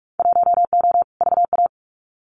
Morse code, heard as a keyed tone, about 44 wpm, 700 Hz. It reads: 1O VA
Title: Winter Field Day Exchanges 45wpm